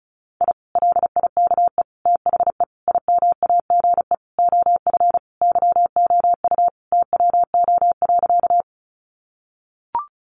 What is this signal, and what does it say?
Morse code, heard as a keyed tone, about 34 wpm, 700 Hz. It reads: I LIKE THE IMAGE OF YOU TWO.
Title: Sentences from Top 700 Words 35wpm